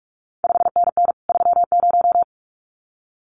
Morse code, heard as a keyed tone, about 46 wpm, 700 Hz. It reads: 5NN 30
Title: CQ WW DX Contest Exchange Encore - 45wpm